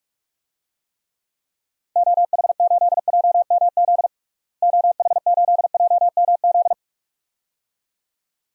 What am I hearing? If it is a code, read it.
Morse code, heard as a keyed tone, about 44 wpm, 700 Hz. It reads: OH8JMZ OH8JMZ
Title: DXCC-100 Call Signs - 6 Characters - Repeated 2x - 45wpm